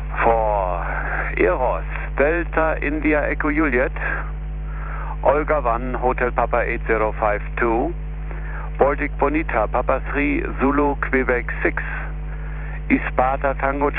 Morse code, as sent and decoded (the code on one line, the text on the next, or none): none